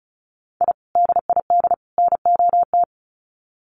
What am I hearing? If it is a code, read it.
Morse code, heard as a keyed tone, about 34 wpm, 700 Hz. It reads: I DID NOT
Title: Sets of 3 Words 35wpm